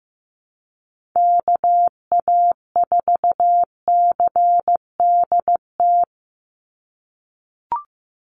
Morse code, heard as a keyed tone, about 15 wpm, 700 Hz. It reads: KA4CDT
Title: USA Call Signs 15wpm